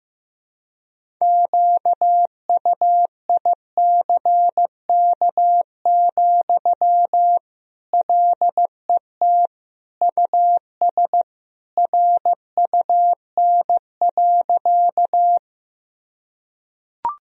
Morse code, heard as a keyed tone, about 15 wpm, 700 Hz. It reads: QUICK, LET US RUN.
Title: Sentences from Top 1000 Words - Review Part 1 - 15wpm